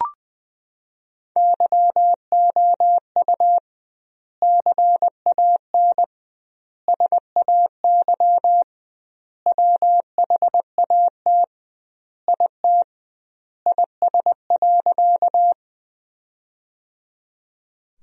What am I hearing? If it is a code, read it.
Morse code, heard as a keyed tone, about 20 wpm, 700 Hz. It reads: YOU CAN SAY WHAT IT IS.
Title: 2x Word-Spacing - Sentences from Top 200 Words - 20wpm